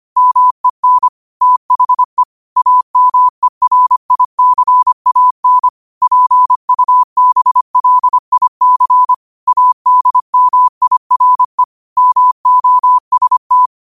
Morse code, heard as a keyed tone, about 25 wpm, 1000 Hz. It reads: MEN THE AMERICAN PUBLIC ADMIRE MOST